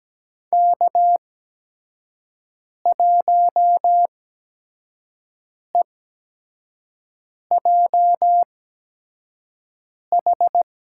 Morse code, heard as a keyed tone, about 17 wpm, spaced as for 5 wpm, 700 Hz. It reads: K1EJH
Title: USA Call Signs - 5 Characters - 8x Character Spacing - 17wpm